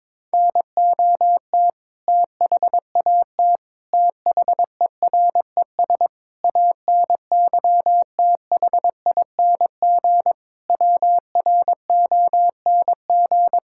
Morse code, heard as a keyed tone, about 22 wpm, 700 Hz. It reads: NOT THAT THERES ANYTHING WRONG
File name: practice / 22wpm / Sets of 7 Words - 22wpm